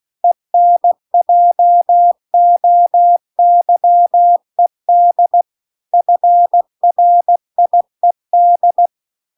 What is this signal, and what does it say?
Morse code, heard as a keyed tone, about 16 wpm, 700 Hz. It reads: ENJOYED FRIED